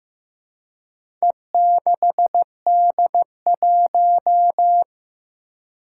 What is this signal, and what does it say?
Morse code, heard as a keyed tone, about 15 wpm, 700 Hz. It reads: E6D1
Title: Four Challenging Letter-Number Combinations 15wpm